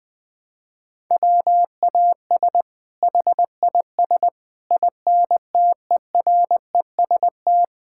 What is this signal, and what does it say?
Morse code, heard as a keyed tone, about 20 wpm, 700 Hz. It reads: WAS HIS INTEREST
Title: Sentences from Top 1000 Words - Review Part 5 - 20wpm